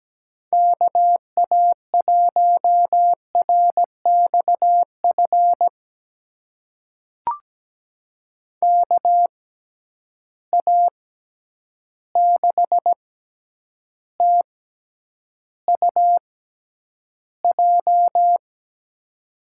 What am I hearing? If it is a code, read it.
Morse code, heard as a keyed tone, about 17 wpm, spaced as for 9 wpm, 700 Hz. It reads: KA1RXF KA6TUJ
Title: USA Call Signs - 6 Characters - 6x Character Spacing - 17wpm